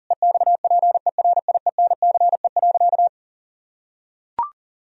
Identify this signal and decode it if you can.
Morse code, heard as a keyed tone, about 40 wpm, 700 Hz. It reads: EXPERIENCE.
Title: Sentences from Top 1000 Words - Review Part 3 - 40wpm